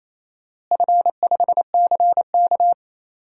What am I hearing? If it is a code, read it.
Morse code, heard as a keyed tone, about 28 wpm, 700 Hz. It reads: F5CK
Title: DXCC-100 Call Signs - 4 Characters - Repeated 2x - 28wpm